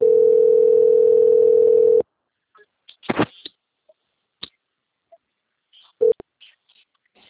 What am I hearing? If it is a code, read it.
Ringback tone (North America), 440 + 480 Hz.